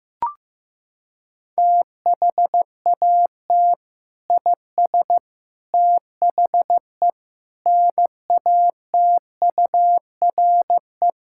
Morse code, heard as a keyed tone, about 15 wpm, 700 Hz. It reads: THAT IS THE NATURE
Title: Sentences from Top 1000 Words - Review Part 5 - 15wpm